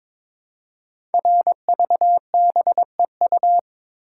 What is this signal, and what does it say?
Morse code, heard as a keyed tone, about 22 wpm, 700 Hz. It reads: RVBEU
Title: Code Groups - 5 Letters - 22wpm